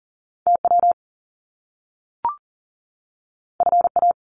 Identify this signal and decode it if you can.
Morse code, heard as a keyed tone, about 42 wpm, 700 Hz. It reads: TW FA
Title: Top Two Letter Combinations in Words 40wpm